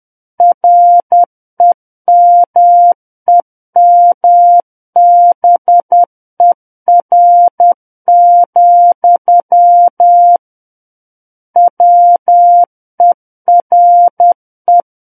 Morse code, heard as a keyed tone, about 10 wpm, 700 Hz. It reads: REMEMBER, WERE
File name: Book One / 01-Chapter 1 - 10 WPM